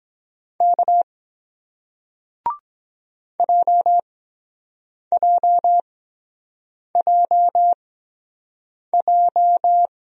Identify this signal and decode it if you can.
Morse code, heard as a keyed tone, about 23 wpm, spaced as for 7 wpm, 700 Hz. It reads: K JJJJ